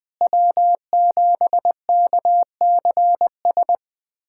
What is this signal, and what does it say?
Morse code, heard as a keyed tone, about 20 wpm, 700 Hz. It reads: W7KCS
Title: USA Callsign DE Callsign - 5 Characters - 20wpm